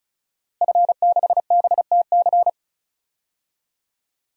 Morse code, heard as a keyed tone, about 34 wpm, 700 Hz. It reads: F6BTC
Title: DXCC-100 Call Signs - 5 Characters - Repeated 2x - 35wpm